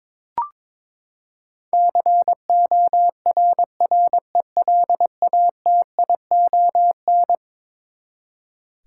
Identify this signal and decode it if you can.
Morse code, heard as a keyed tone, about 21 wpm, 700 Hz. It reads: CORRELATION
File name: practice / 22wpm / Eleven Letter Words - 22wpm